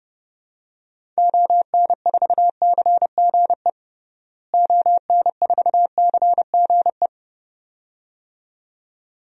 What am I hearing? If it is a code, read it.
Morse code, heard as a keyed tone, about 32 wpm, 700 Hz. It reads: ON4CGE ON4CGE
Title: DXCC-100 Call Signs - 6 Characters - Repeated 2x - 30wpm